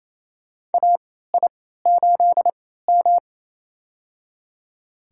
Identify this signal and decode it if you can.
Morse code, heard as a keyed tone, about 28 wpm, spaced as for 16 wpm, 700 Hz. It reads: AI8M
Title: USA Call Signs - 4 Characters - 3x Character Spacing - 28wpm